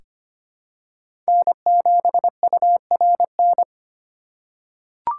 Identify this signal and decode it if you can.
Morse code, heard as a keyed tone, about 25 wpm, 700 Hz. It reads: N7URN
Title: USA Call Signs - 5 Characters - 6x Character Spacing - 25wpm